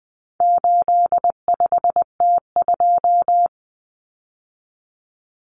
Morse code, heard as a keyed tone, about 20 wpm, 700 Hz. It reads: 85T2
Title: Common Cut Numbers - 4-Digit 20wpm